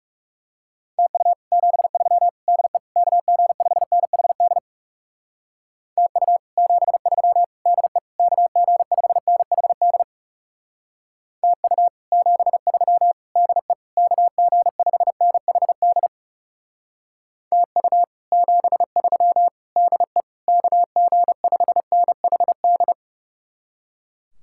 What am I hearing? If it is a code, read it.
Morse code, heard as a keyed tone, about 39 wpm, 700 Hz. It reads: TU 73 DE KG5NHD TU 73 DE KG5NHD TU 73 DE KG5NHD TU 73 DE KG5NHD